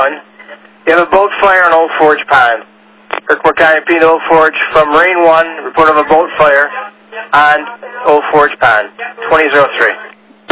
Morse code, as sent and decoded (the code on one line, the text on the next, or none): none